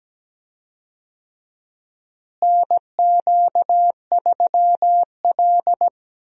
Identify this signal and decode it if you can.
Morse code, heard as a keyed tone, about 17 wpm, 700 Hz. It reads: NQ3L
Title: USA Call Signs - 4 Characters - Repeated 2x - 17wpm